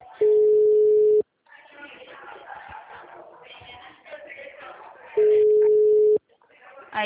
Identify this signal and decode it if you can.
Ringback tone (Europe, 425 Hz), with speech.
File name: out- -2008- - - .319